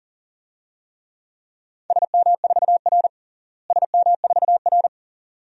Morse code, heard as a keyed tone, about 40 wpm, 700 Hz. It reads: SM4R SM4R
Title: DXCC-100 Call Signs - 4 Characters - Repeated 2x - 40wpm